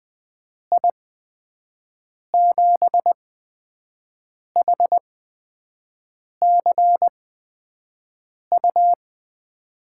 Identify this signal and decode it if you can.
Morse code, heard as a keyed tone, about 20 wpm, spaced as for 5 wpm, 700 Hz. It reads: I7HCU
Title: USA Call Signs - 6 Characters - 8x Character Spacing - 20wpm